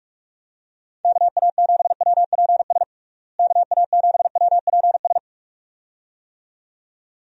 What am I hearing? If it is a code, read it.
Morse code, heard as a keyed tone, about 46 wpm, 700 Hz. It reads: KA7WPS KA7WPS
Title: USA Call Signs - 6 Characters - Repeated 2x - 45wpm